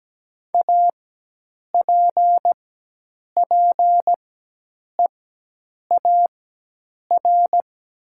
Morse code, heard as a keyed tone, about 17 wpm, spaced as for 8 wpm, 700 Hz. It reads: APPEAR
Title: 4x Character Spacing - Top 500 Words - 17wpm